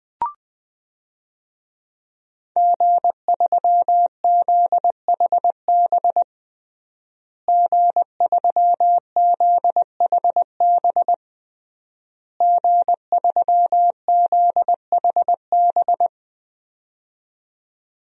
Morse code, heard as a keyed tone, about 20 wpm, 700 Hz. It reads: G3ZHB G3ZHB G3ZHB
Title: DXCC-100 Call Signs - 5 Characters - Repeated 3x - 20wpm